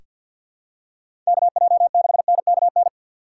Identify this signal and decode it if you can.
Morse code, heard as a keyed tone, about 47 wpm, 700 Hz. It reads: KJ6NKN
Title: USA Call Signs - 6 Characters - 6x Character Spacing - 50wpm